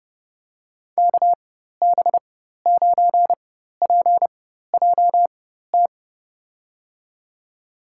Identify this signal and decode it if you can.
Morse code, heard as a keyed tone, about 31 wpm, spaced as for 14 wpm, 700 Hz. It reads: KB9PJT
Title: USA Call Signs - 6 Characters - 4x Character Spacing - 30wpm